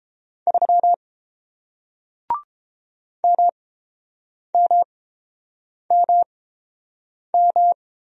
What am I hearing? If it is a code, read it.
Morse code, heard as a keyed tone, about 31 wpm, spaced as for 7 wpm, 700 Hz. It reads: 3 MMMM